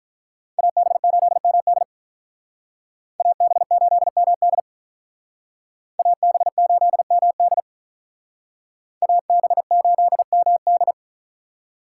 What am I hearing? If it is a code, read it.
Morse code, heard as a keyed tone, about 46 wpm, 700 Hz. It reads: AB8MD AB8MD AB8MD AB8MD